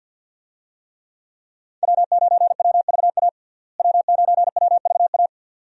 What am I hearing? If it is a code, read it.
Morse code, heard as a keyed tone, about 53 wpm, 700 Hz. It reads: W9WVA W9WVA
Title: USA Call Signs - 5 Characters - Repeated 2x - 50wpm